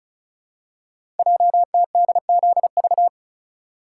Morse code, heard as a keyed tone, about 36 wpm, 700 Hz. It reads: JTDZV
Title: Code Groups - 5 Letters - 35wpm